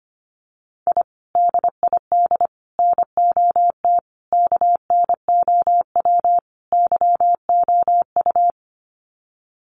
Morse code, heard as a keyed tone, about 25 wpm, 700 Hz. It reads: I DID NOT KNOW YOU
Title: Sets of 5 Words Encore2 25wpm